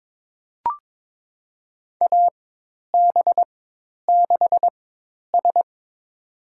Morse code, heard as a keyed tone, about 22 wpm, spaced as for 10 wpm, 700 Hz. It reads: AB6S
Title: USA Call Signs - 4 Characters - 4x Character Spacing - 22wpm